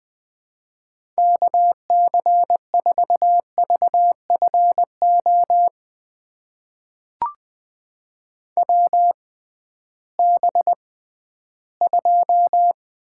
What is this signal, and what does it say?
Morse code, heard as a keyed tone, about 20 wpm, spaced as for 13 wpm, 700 Hz. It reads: KC4VFO WB2
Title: USA Call Signs - 6 Characters - 6x Character Spacing - 20wpm